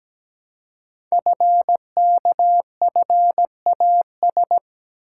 Morse code, heard as a keyed tone, about 17 wpm, 700 Hz. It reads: FKFAS
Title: Code Groups - 5 Letter:Numbers - 17wpm